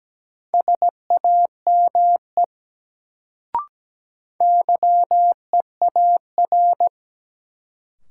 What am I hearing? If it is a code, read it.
Morse code, heard as a keyed tone, about 17 wpm, 700 Hz. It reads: SAME YEAR
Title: Sets of 2 Words - New Word Drill - 17wpm